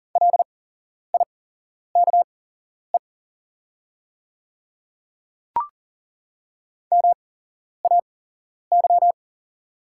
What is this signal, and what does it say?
Morse code, heard as a keyed tone, about 40 wpm, spaced as for 11 wpm, 700 Hz. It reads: LIKE MAY
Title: 8x Character Spacing - Top 200 Words - 40wpm